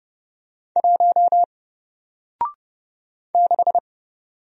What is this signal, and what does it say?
Morse code, heard as a keyed tone, about 30 wpm, 700 Hz. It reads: 1 6
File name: Single-Letter-Number-30wpm